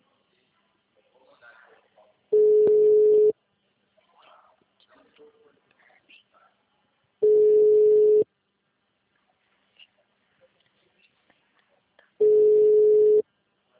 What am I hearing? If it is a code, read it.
Ringback tone (Europe, 425 Hz), with speech.
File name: out- -2005- - - .61823